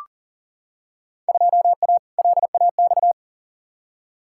Morse code, heard as a keyed tone, about 40 wpm, 700 Hz. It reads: 2A LAX
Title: Field Day Exchanges - 40wpm